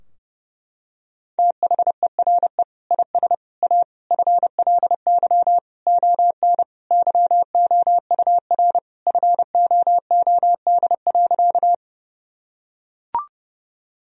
Morse code, heard as a keyed tone, about 30 wpm, 700 Hz. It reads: THERE IS A FLY ON YOUR FOOD.
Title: Sentences from Top 1000 Words - Review Part 2 - 30wpm